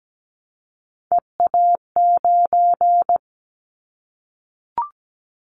Morse code, heard as a keyed tone, about 17 wpm, 700 Hz. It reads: EA9
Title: Callsign Prefixes for Top 100 DXCC - 17wpm